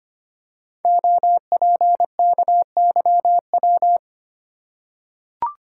Morse code, heard as a keyed tone, about 25 wpm, 700 Hz. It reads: OPKYW
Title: Code Groups - 5 Letter:Numbers - 25wpm